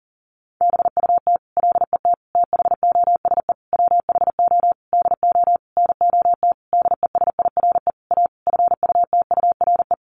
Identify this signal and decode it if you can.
Morse code, heard as a keyed tone, about 40 wpm, 700 Hz. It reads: BUT LET THOSE WHO DO NOT DESIRE A FUTURE
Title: Sentences from Top 1500 Words 40wpm